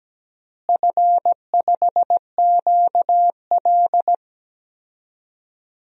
Morse code, heard as a keyed tone, about 17 wpm, 700 Hz. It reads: F5QL
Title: DXCC-100 Call Signs - 4 Characters - Repeated 3x - 17wpm